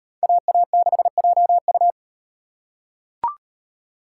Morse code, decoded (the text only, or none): AA6JU